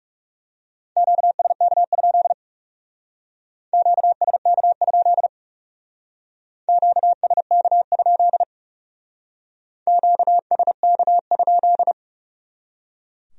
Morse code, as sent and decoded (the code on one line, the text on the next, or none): --.- ... -.- ..--.. / --.- ... -.- ..--.. / --.- ... -.- ..--.. / --.- ... -.- ..--..
QSK? QSK? QSK? QSK?